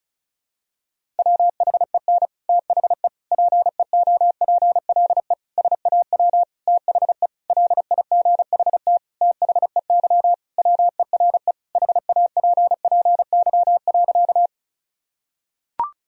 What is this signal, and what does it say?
Morse code, heard as a keyed tone, about 36 wpm, 700 Hz. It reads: WHEN THE PEOPLE SAW THE LIGHT THEY WERE HAPPY.
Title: Sentences from Top 1000 Words - Review Part 1 - 35wpm